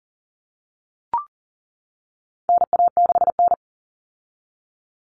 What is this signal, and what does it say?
Morse code, heard as a keyed tone, about 40 wpm, 700 Hz. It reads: NA6N
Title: Common Cut Numbers - 4-Digit 40wpm